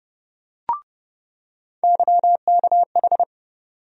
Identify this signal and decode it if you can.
Morse code, heard as a keyed tone, about 30 wpm, 700 Hz. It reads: YKH
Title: USA Call Sign Postfixes - 3 Letters 30wpm